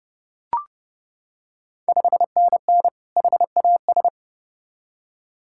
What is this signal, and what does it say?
Morse code, heard as a keyed tone, about 30 wpm, 700 Hz. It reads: 5NN HAS
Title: Ontario QSO Party Exchange 30wpm